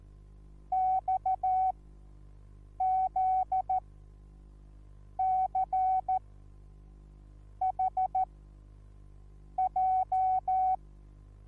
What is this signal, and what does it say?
Morse code, heard as a keyed tone, about 14 wpm, spaced as for 5 wpm, 750 Hz. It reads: XZCHJ
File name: Onlinelehrgang / morsen / DJ4UF Morsen21